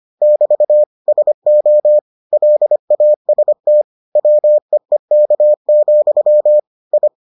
Morse code, heard as a keyed tone, about 25 wpm, 600 Hz. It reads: <BT> SO LAST WEEK, I